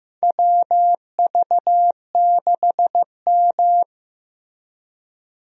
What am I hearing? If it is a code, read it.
Morse code, heard as a keyed tone, about 15 wpm, 700 Hz. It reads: WV6M
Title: USA Call Signs - 4 Characters - Repeated 3x - 15wpm